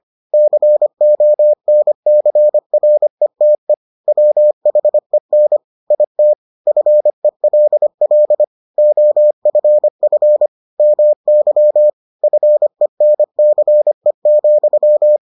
Morse code, heard as a keyed tone, about 25 wpm, 600 Hz. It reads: CONCRETE WHEN IT FELL OFF MY FENCE,